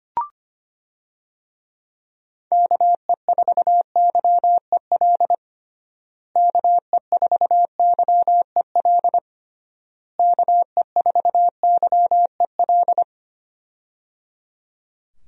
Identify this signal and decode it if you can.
Morse code, heard as a keyed tone, about 25 wpm, 700 Hz. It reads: KE4YEL KE4YEL KE4YEL
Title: USA Call Signs - 6 Characters - Repeated 3x - 25wpm